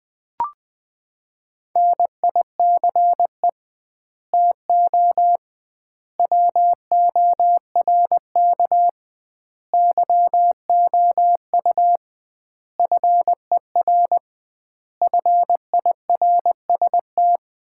Morse code, heard as a keyed tone, about 20 wpm, 700 Hz. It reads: NICE TO WORK YOU FER FIRST
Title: Common QSO Phrases - 2x Word-Spacing - 20wpm